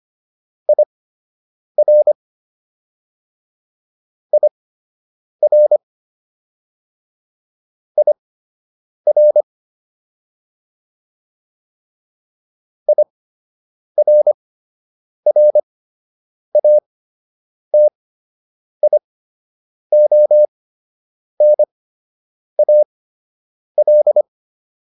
Morse code, heard as a keyed tone, about 25 wpm, spaced as for 8 wpm, 600 Hz. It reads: IR IR IR IRRATIONAL